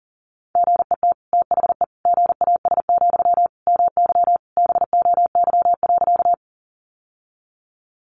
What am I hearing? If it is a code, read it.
Morse code, heard as a keyed tone, about 40 wpm, 700 Hz. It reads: GET THE GAS, MY BOY.
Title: Sentences from Top 1100 Words 40wpm